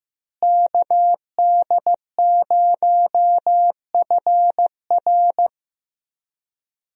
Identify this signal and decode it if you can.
Morse code, heard as a keyed tone, about 15 wpm, 700 Hz. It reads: KD0FR
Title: USA Call Signs - 5 Characters - Repeated 3x - 15wpm